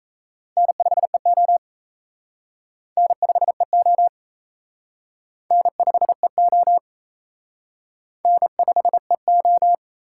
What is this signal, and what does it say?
Morse code, heard as a keyed tone, about 36 wpm, 700 Hz. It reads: N5EO N5EO N5EO N5EO